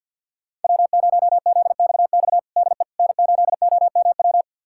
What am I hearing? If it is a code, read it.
Morse code, heard as a keyed tone, about 48 wpm, 700 Hz. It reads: W0ZXX DE N8OMW
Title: USA Callsign DE Callsign - 5 Characters - 50wpm